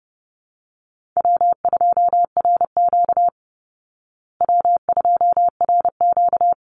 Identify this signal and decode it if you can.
Morse code, heard as a keyed tone, about 30 wpm, 700 Hz. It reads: W2RQ W2RQ
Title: Call Signs Repeated 3x - 30wpm